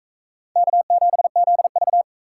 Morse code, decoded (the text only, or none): K7ZV